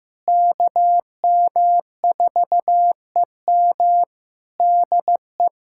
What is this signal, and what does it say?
Morse code, heard as a keyed tone, about 15 wpm, 700 Hz. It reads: KM4EM DE
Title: Practice QSO - Exchange 4 - Ending - 15wpm